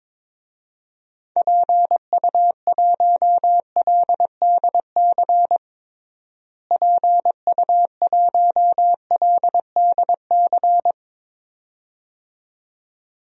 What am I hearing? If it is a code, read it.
Morse code, heard as a keyed tone, about 22 wpm, 700 Hz. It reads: PU1LDC PU1LDC
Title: DXCC-100 Call Signs - 6 Characters - Repeated 2x - 22wpm